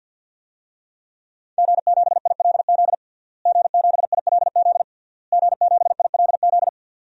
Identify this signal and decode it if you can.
Morse code, heard as a keyed tone, about 51 wpm, 700 Hz. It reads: G7ILZ G7ILZ G7ILZ
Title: DXCC-100 Call Signs - 5 Characters - Repeated 3x - 50wpm